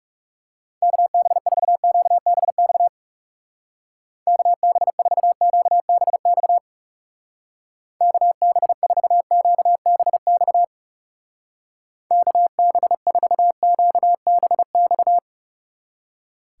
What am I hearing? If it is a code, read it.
Morse code, heard as a keyed tone, about 38 wpm, 700 Hz. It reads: KB4QBX KB4QBX KB4QBX KB4QBX